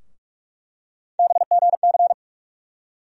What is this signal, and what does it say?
Morse code, heard as a keyed tone, about 43 wpm, 700 Hz. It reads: BGC
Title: USA Call Sign Postfixes - 3 Letters 45wpm